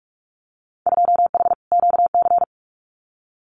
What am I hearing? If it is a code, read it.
Morse code, heard as a keyed tone, about 44 wpm, 700 Hz. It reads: 2H QC
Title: Winter Field Day Exchanges 45wpm